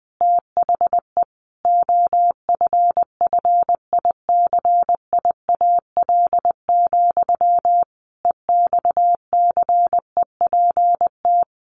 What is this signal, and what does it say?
Morse code, heard as a keyed tone, about 20 wpm, 700 Hz. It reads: THE OFFICIAL, EXCEPT
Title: A Sound Of Thunder 20wpm